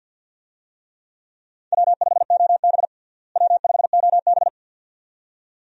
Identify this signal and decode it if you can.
Morse code, heard as a keyed tone, about 51 wpm, 700 Hz. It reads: W5OB W5OB
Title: USA Call Signs - 4 Characters - Repeated 2x - 50wpm